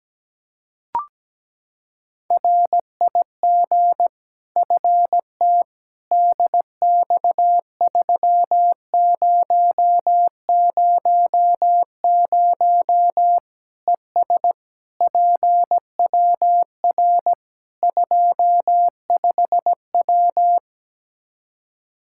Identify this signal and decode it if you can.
Morse code, heard as a keyed tone, about 17 wpm, 700 Hz. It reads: RIG FT DX3000 ES PWR 25W
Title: Practice QSO - Exchange 2 - Info - 17wpm